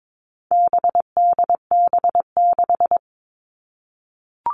Morse code, heard as a keyed tone, about 22 wpm, 700 Hz. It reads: BDB6
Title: All Cut Numbers - 4-Digit 22wpm